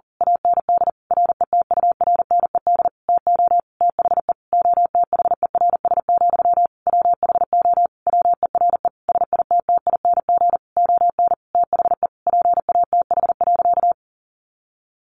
Morse code, heard as a keyed tone, about 37 wpm, 700 Hz. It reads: AND RETURNED TO THE OTHERS, WHO WERE SITTING ON THE PATH.
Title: A Sound Of Thunder 40wpm